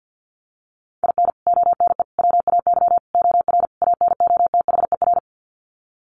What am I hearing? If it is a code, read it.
Morse code, heard as a keyed tone, about 50 wpm, 700 Hz. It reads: IN ONE WAY OR ANOTHER
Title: Sets of 5 Words Encore 50wpm